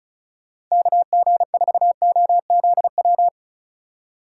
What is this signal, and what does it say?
Morse code, heard as a keyed tone, about 34 wpm, 700 Hz. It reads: KG4OZW
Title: USA Call Signs - 6 Characters - 4x Character Spacing - 35wpm